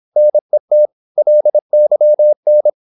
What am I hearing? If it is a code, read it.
Morse code, heard as a keyed tone, about 26 wpm, 600 Hz. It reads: NET LYN
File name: Korte ord DK 26wpm